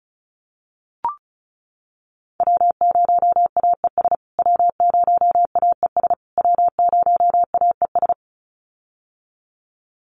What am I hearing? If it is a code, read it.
Morse code, heard as a keyed tone, about 34 wpm, 700 Hz. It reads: W0AES W0AES W0AES
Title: USA Call Signs Repeated 3x 35wpm